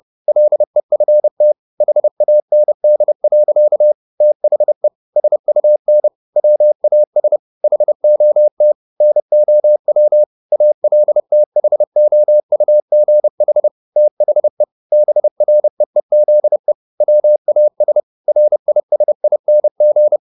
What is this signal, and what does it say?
Morse code, heard as a keyed tone, about 30 wpm, 600 Hz. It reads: LEFT HAND. THE SUN WAS HOT NOW ALTHOUGH THE BREEZE WAS RISING
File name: the old man and the sea / the old man and the sea 1 30wpm